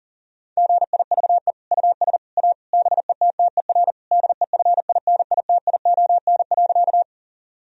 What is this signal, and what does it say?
Morse code, heard as a keyed tone, about 40 wpm, 690 Hz. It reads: GIVE US A BETTER DEFINITION.
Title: Sentences from Top 1000 Words - Review Part 4 - 40wpm